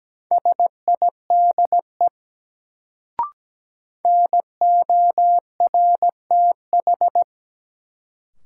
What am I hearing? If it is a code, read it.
Morse code, heard as a keyed tone, about 17 wpm, 700 Hz. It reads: SIDE NORTH
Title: Sentences from Top 400 Words - New Word Drill - 17wpm